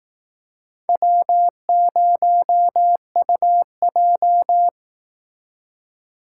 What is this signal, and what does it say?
Morse code, heard as a keyed tone, about 18 wpm, 700 Hz. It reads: W0UJ
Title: Speed Racing - USA Call Signs 15wpm